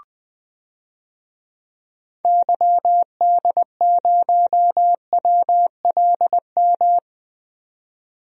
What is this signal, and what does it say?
Morse code, heard as a keyed tone, about 20 wpm, 700 Hz. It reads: YD0WLM
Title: DXCC-100 Call Signs - 6 Characters - 20wpm